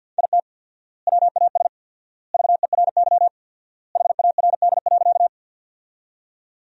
Morse code, heard as a keyed tone, about 48 wpm, 700 Hz. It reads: IT WAS VERY HARD.
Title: 4x Word-Spacing - Sentences from Top 300 Words - 50wpm